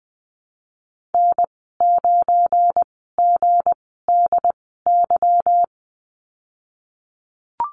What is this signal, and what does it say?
Morse code, heard as a keyed tone, about 19 wpm, spaced as for 14 wpm, 700 Hz. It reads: N9GDY
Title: 186 Lesson 30g - Letter b - Call Signs 20wpm